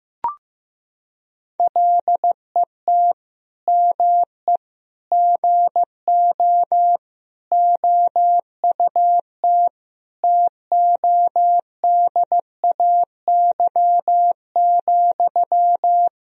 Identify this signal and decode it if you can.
Morse code, heard as a keyed tone, about 15 wpm, 700 Hz. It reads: LET ME GO OUT TODAY,
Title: Sentences from Top 1000 Words - Review Part 4 - 15wpm